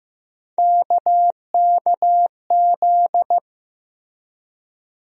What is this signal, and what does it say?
Morse code, heard as a keyed tone, about 15 wpm, 700 Hz. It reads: KKZ
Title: USA Call Sign Postfixes - 3 Letters 15wpm